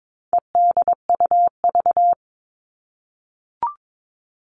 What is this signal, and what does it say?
Morse code, heard as a keyed tone, about 22 wpm, 700 Hz. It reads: EDUV
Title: All Cut Numbers - 4-Digit 22wpm